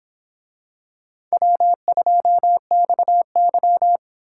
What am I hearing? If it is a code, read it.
Morse code, heard as a keyed tone, about 26 wpm, 700 Hz. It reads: W2XY